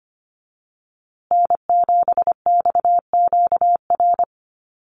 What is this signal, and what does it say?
Morse code, heard as a keyed tone, about 25 wpm, 700 Hz. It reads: N7XQR
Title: USA Call Signs Repeated 2x 25wpm